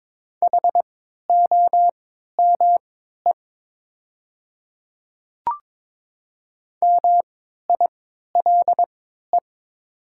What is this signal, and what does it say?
Morse code, heard as a keyed tone, about 22 wpm, spaced as for 13 wpm, 690 Hz. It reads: HOME MILE
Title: 3x Character Spacing - Top 300 Words - 22wpm